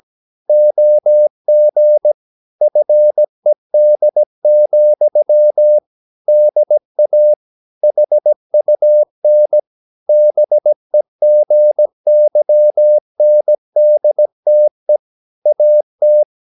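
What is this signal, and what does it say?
Morse code, heard as a keyed tone, about 17 wpm, 600 Hz. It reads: OG FED, DA HUN BEGYNDTE AT
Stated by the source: Slow (QRS) Morse practice at 17 wpm, from CW med Gnister.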